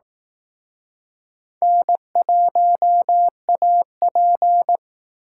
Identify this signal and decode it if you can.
Morse code, heard as a keyed tone, about 18 wpm, 700 Hz. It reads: N1AP
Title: Speed Racing - USA 1x2 Call Signs 15wpm